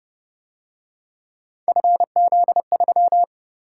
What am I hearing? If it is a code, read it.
Morse code, heard as a keyed tone, about 29 wpm, 700 Hz. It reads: FZ3